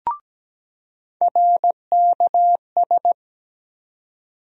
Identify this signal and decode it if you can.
Morse code, heard as a keyed tone, about 17 wpm, 700 Hz. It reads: RKS
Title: USA Call Sign Postfixes - 3 Letters 17wpm